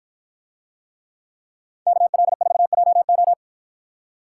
Morse code, heard as a keyed tone, about 50 wpm, 700 Hz. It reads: KL4JO